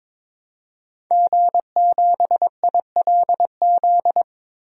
Morse code, heard as a keyed tone, about 22 wpm, 700 Hz. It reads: G7ILZ
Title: DXCC-100 Call Signs - 5 Characters - Repeated 3x - 22wpm